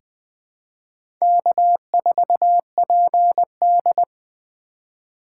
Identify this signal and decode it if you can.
Morse code, heard as a keyed tone, about 21 wpm, 700 Hz. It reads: K4PD
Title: USA Call Signs - 4 Characters - 6x Character Spacing - 20wpm